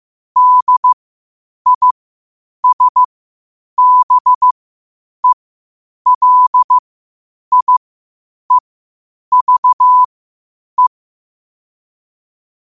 Morse code, heard as a keyed tone, about 15 wpm, spaced as for 8 wpm, 1000 Hz. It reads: DISBELIEVE
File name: 8wpm